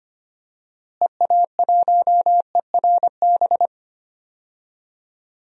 Morse code, heard as a keyed tone, about 25 wpm, 700 Hz. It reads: EA1ERB
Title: DXCC-100 Call Signs - 6 Characters - Repeated 3x - 25wpm